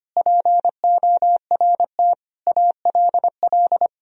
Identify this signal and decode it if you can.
Morse code, heard as a keyed tone, about 25 wpm, 700 Hz. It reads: PORT ALL
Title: Sentences from Top 1000 Words - Review Part 1 - 25wpm